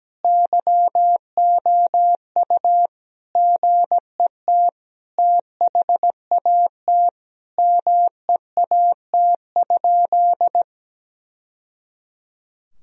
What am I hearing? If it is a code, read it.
Morse code, heard as a keyed tone, about 17 wpm, 700 Hz. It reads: YOU GET THAT MEAT?
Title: Sentences from Top 1000 Words - Review Part 3 - 17wpm